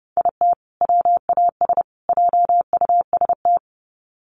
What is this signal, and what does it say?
Morse code, heard as a keyed tone, about 30 wpm, 700 Hz. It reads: IT WAS JUST
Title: Sets of 3 Words 30wpm